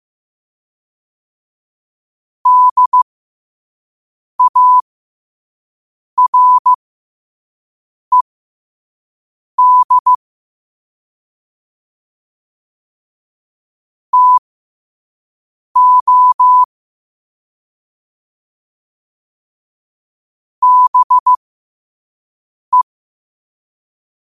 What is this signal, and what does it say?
Morse code, heard as a keyed tone, about 15 wpm, spaced as for 5 wpm, 1000 Hz. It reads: DARED TO BE
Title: 5 WPM morse code quotes for Fri, 01 Mar 2024